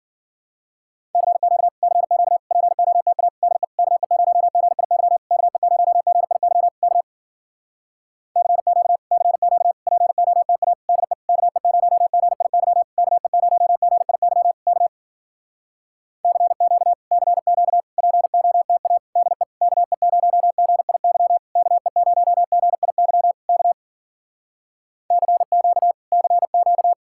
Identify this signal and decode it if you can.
Morse code, heard as a keyed tone, about 53 wpm, 700 Hz. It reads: CQ CQ POTA DE KE0GIY KE0GIY K CQ CQ POTA DE KE0GIY KE0GIY K CQ CQ POTA DE KE0GIY KE0GIY K CQ CQ